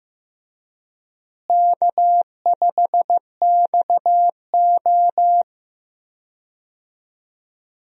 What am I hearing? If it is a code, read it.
Morse code, heard as a keyed tone, about 16 wpm, 700 Hz. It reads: K5XO
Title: USA Call Signs - 4 Characters - Repeated 3x - 15wpm